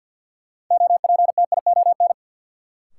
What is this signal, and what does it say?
Morse code, heard as a keyed tone, about 52 wpm, 700 Hz. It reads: OPTION
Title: Sentences from Top 1300 Words - New Word Drill - 50wpm